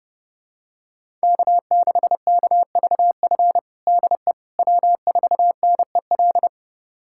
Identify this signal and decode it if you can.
Morse code, heard as a keyed tone, about 30 wpm, 700 Hz. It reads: K6KVF DE W4NEL
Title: USA Callsign DE Callsign - 5 Characters - 30wpm